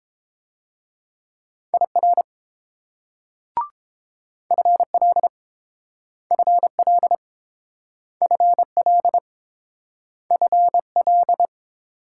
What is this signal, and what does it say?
Morse code, heard as a keyed tone, about 30 wpm, 700 Hz. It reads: IR FL FL FL FL